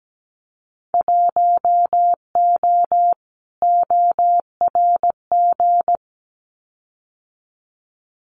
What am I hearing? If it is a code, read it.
Morse code, heard as a keyed tone, about 17 wpm, 700 Hz. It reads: 1O ORG
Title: Winter Field Day Exchanges 17wpm